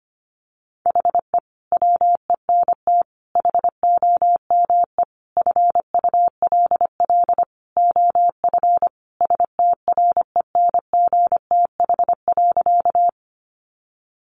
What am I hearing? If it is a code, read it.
Morse code, heard as a keyed tone, about 25 wpm, 700 Hz. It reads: HE WENT HOME FULL OF STRENGTH.
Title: Sentences from Top 1300 Words 25wpm